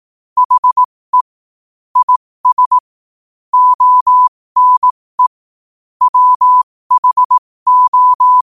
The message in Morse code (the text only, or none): HE IS ONE WHO